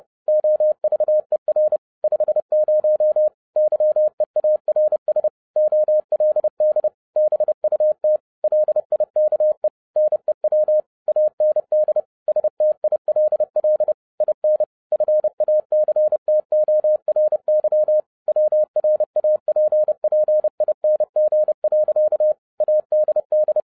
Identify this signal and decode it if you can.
Morse code, heard as a keyed tone, about 30 wpm, 610 Hz. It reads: OVER 50 YEARS OLD BUT LIKE NEW AND STILL IN FACTORY WRAPPING. ADD
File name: 2011-10-mp3 / n10-Czech Army Morse Keys 30wpm